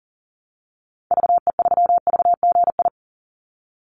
Morse code, heard as a keyed tone, about 40 wpm, 700 Hz. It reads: VE3VGI
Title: Call Signs Repeated 3x - 40wpm